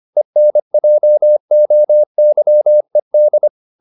Morse code, heard as a keyed tone, about 25 wpm, 600 Hz. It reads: ENJOYED